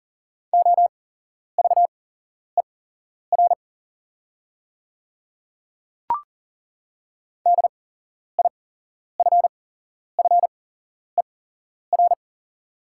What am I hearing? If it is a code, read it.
Morse code, heard as a keyed tone, about 40 wpm, spaced as for 11 wpm, 690 Hz. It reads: OVER DIFFER
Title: 8x Character Spacing - Top 300 Words - 40wpm